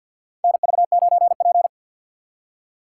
Morse code, heard as a keyed tone, about 52 wpm, 700 Hz. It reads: NV9P
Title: USA Call Signs - 4 Characters - 3x Character Spacing - 50wpm